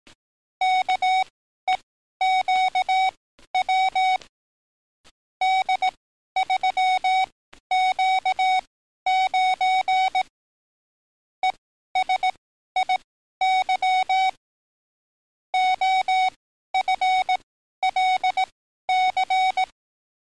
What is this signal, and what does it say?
Morse code, heard as a keyed tone, about 18 wpm, spaced as for 12 wpm, 750 Hz. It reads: KEQW D3Q9 ESIY OFLC
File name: training / NAVPERS 11049to11060 / NAVPERS 11053 RA RB